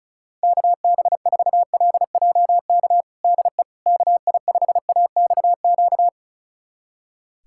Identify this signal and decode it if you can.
Morse code, heard as a keyed tone, about 36 wpm, 700 Hz. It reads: KB4LJK DE KI5AXQ
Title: USA Callsign DE Callsign - 6 Characters - 35wpm